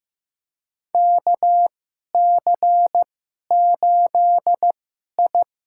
Morse code, heard as a keyed tone, about 15 wpm, spaced as for 11 wpm, 700 Hz. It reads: KC8I
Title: USA Call Signs - 2x Character Spacing - 15wpm